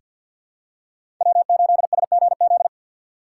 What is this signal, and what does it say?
Morse code, heard as a keyed tone, about 48 wpm, 700 Hz. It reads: W8SGZ
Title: USA Call Signs - 4x Character Spacing - 50wpm